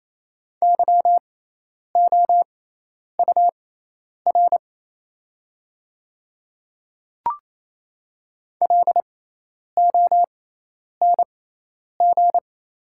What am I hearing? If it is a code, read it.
Morse code, heard as a keyed tone, about 28 wpm, spaced as for 10 wpm, 700 Hz. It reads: YOUR LONG
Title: 6x Character Spacing - Top 100 Words - 28wpm